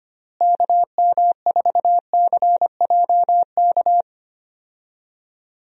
Morse code, heard as a keyed tone, about 25 wpm, 700 Hz. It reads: KM4CJK
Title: USA Call Signs - 6 Characters - Repeated 3x - 25wpm